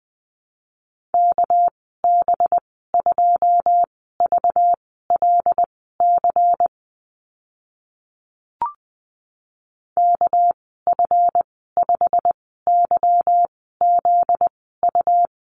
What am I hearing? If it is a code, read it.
Morse code, heard as a keyed tone, about 20 wpm, spaced as for 14 wpm, 700 Hz. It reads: KB2VLC KF5YZU
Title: 245 Lesson 39g - Letter z - Call Signs 20wpm